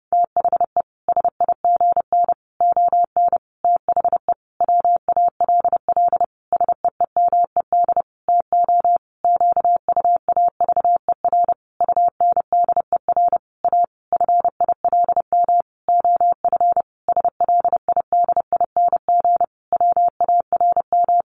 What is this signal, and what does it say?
Morse code, heard as a keyed tone, about 30 wpm, 700 Hz. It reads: THE SIGN ON THE WALL SEEMED TO QUAVER UNDER A FILM OF SLIDING WARM
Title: A Sound Of Thunder 30wpm